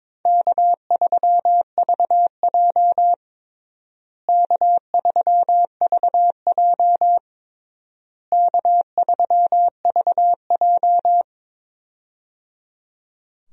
Morse code, heard as a keyed tone, about 22 wpm, 700 Hz. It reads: K3VJ K3VJ K3VJ
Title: USA Call Signs - 4 Characters - Repeated 3x - 22wpm